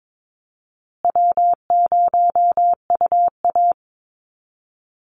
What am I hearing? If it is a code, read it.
Morse code, heard as a keyed tone, about 22 wpm, 700 Hz. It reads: W0UA
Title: Easy Call Signs Repeated 3x - 22wpm